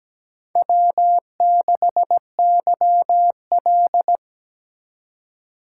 Morse code, heard as a keyed tone, about 17 wpm, 700 Hz. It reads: W6YL
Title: USA Call Signs - 2x Character Spacing - 17wpm